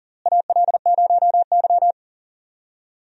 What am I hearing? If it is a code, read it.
Morse code, heard as a keyed tone, about 40 wpm, 700 Hz. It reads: AL0Y
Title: USA Call Signs - 4 Characters - Repeated 3x - 40wpm